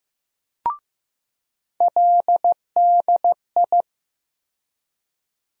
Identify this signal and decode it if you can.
Morse code, heard as a keyed tone, about 15 wpm, 700 Hz. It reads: LDI
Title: USA Call Sign Postfixes - 3 Letters 15wpm